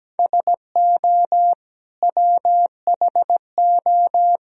Morse code, heard as a keyed tone, about 17 wpm, 700 Hz. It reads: SO WHO
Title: Sentences from Top 1000 Words - Review Part 3 - 17wpm